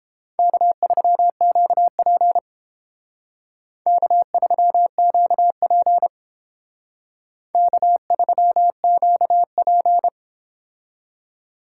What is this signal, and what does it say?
Morse code, heard as a keyed tone, about 31 wpm, 700 Hz. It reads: K3QP K3QP K3QP